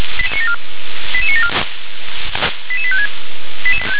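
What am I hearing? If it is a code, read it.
Five-tone selective call (ZVEI), heard as quick runs of five tones, tones of about 80 ms. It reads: 88974 88856
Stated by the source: Received on SDR, ZVEI five-tone code 88910, 357 channel.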